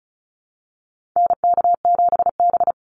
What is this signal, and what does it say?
Morse code, heard as a keyed tone, about 34 wpm, 700 Hz. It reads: NK7B
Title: Call Signs Repeated 3x - 35wpm